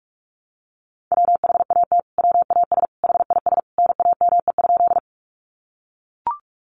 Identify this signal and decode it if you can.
Morse code, heard as a keyed tone, about 44 wpm, 700 Hz. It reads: WHAT WAS HIS NAME?
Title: Sentences from Top 200 Words 45wpm